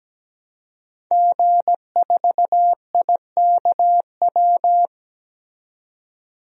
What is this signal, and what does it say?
Morse code, heard as a keyed tone, about 17 wpm, 700 Hz. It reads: G4IKW
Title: DXCC-100 Call Signs - 5 Characters - 17wpm